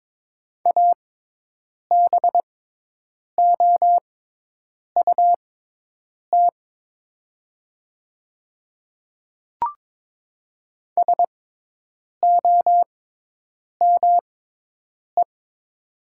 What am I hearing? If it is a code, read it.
Morse code, heard as a keyed tone, about 22 wpm, spaced as for 8 wpm, 700 Hz. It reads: ABOUT SOME
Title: 6x Character Spacing - Top 100 Words - 22wpm